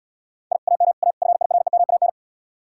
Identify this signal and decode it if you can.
Morse code, heard as a keyed tone, about 79 wpm, 700 Hz. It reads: I AM A FIREMAN